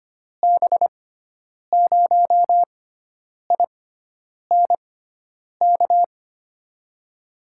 Morse code, heard as a keyed tone, about 25 wpm, spaced as for 9 wpm, 700 Hz. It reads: B0INK
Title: USA Call Signs - 6x Character Spacing - 25wpm